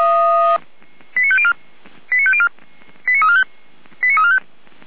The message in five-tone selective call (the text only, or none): 89504 89504 89345 89345